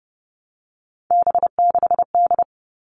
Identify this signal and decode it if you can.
Morse code, heard as a keyed tone, about 30 wpm, 700 Hz. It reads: B6D
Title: All Cut Numbers - 3-Digit 30wpm